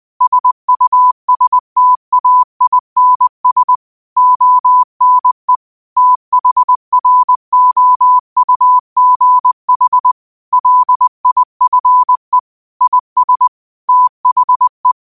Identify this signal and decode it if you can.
Morse code, heard as a keyed tone, about 20 wpm, 1000 Hz. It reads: SUSTAINS ONE THROUGH LIFE IS THE
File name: qotdfiles / 20wpm